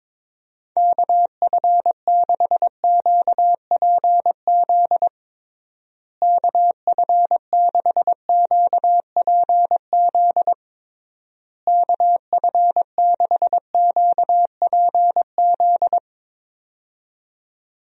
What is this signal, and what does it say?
Morse code, heard as a keyed tone, about 22 wpm, 700 Hz. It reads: KF6QPZ KF6QPZ KF6QPZ
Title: USA Call Signs - 6 Characters - Repeated 3x - 22wpm